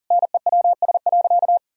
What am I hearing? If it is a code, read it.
Morse code, heard as a keyed tone, about 40 wpm, 700 Hz. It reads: NEWS.
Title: Sentences from Top 1000 Words - Review Part 3 - 40wpm